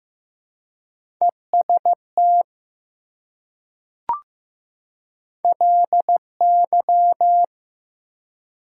Morse code, heard as a keyed tone, about 15 wpm, 700 Hz. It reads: EST LY
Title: Prefixes and Suffixes 15wpm